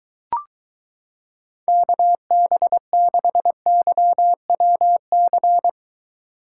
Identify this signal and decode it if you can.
Morse code, heard as a keyed tone, about 23 wpm, 700 Hz. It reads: KB6YWC